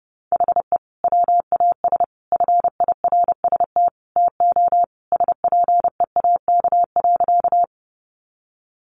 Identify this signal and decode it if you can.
Morse code, heard as a keyed tone, about 30 wpm, 700 Hz. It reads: HE WAS FIRST TO SPEAK.
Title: Sentences from Top 1100 Words 30wpm